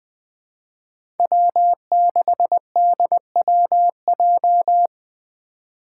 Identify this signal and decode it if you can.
Morse code, heard as a keyed tone, about 20 wpm, 700 Hz. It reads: W6DWJ
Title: Five Challenging Letter-Number Combinations 20wpm